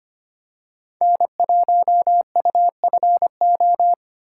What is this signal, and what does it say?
Morse code, heard as a keyed tone, about 25 wpm, 700 Hz. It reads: N1UFO
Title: USA Call Signs - 5 Characters - 4x Character Spacing - 25wpm